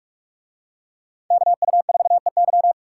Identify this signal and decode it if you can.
Morse code, heard as a keyed tone, about 43 wpm, 700 Hz. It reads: KU4EY